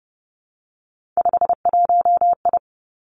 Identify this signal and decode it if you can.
Morse code, heard as a keyed tone, about 30 wpm, 700 Hz. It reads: 51I
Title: Three Challenging Letter-Number Combinations 30wpm